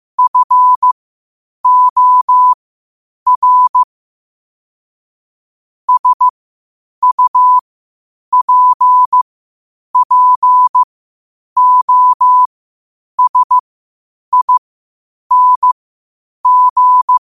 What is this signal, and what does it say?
Morse code, heard as a keyed tone, about 15 wpm, spaced as for 8 wpm, 1000 Hz. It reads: FOR SUPPOSING
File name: qotdfiles / 8wpm